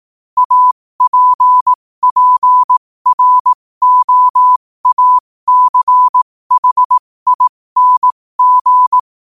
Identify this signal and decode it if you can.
Morse code, heard as a keyed tone, about 18 wpm, spaced as for 15 wpm, 1000 Hz. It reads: APPROACHING
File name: qotdfiles / 15wpm